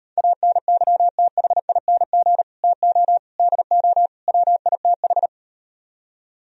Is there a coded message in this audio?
Yes: Morse code from 0.2 to 5.3 s, heard as a keyed tone, about 38 wpm, 700 Hz. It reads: ANYTHING TO DO WITH